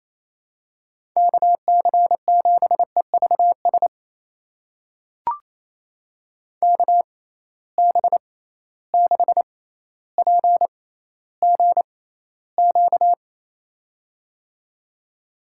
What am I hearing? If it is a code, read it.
Morse code, heard as a keyed tone, about 28 wpm, spaced as for 14 wpm, 700 Hz. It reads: KC7EVS KB6PGQ
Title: USA Call Signs - 6 Characters - 6x Character Spacing - 28wpm